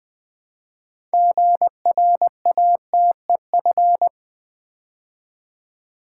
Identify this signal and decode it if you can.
Morse code, heard as a keyed tone, about 20 wpm, 700 Hz. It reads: GRATEF
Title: Build A Word - 2 Letters At A Time - 8-Letter Words - 20wpm